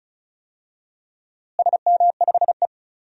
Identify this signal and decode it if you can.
Morse code, heard as a keyed tone, about 36 wpm, 700 Hz. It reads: SM5E
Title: DXCC-100 Call Signs - 4 Characters - Repeated 2x - 35wpm